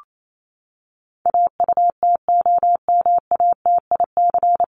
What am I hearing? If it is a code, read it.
Morse code, heard as a keyed tone, about 28 wpm, 700 Hz. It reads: AUTOMATIC
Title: Top 1751-2000 Words 28wpm